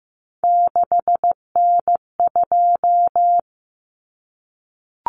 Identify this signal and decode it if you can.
Morse code, heard as a keyed tone, about 15 wpm, 700 Hz. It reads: 6N2
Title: Common Cut Numbers - 3-Digit 15wpm